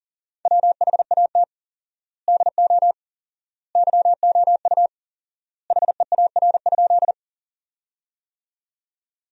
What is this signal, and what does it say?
Morse code, heard as a keyed tone, about 40 wpm, 700 Hz. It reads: WHAT DO YOU HEAR?
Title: 4x Word-Spacing - Sentences from Top 300 Words - 40wpm